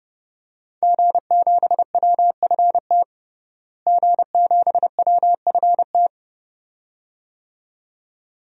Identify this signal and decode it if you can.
Morse code, heard as a keyed tone, about 31 wpm, 700 Hz. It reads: G7WFT G7WFT
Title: DXCC-100 Call Signs - 5 Characters - Repeated 2x - 30wpm